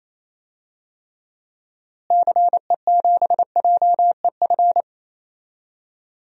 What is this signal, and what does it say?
Morse code, heard as a keyed tone, about 28 wpm, 700 Hz. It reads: CE7JEF
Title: DXCC-100 Call Signs - 6 Characters - Repeated 2x - 28wpm